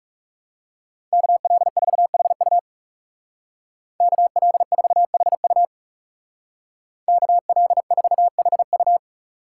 Morse code, heard as a keyed tone, about 40 wpm, 700 Hz. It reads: KL4HU KL4HU KL4HU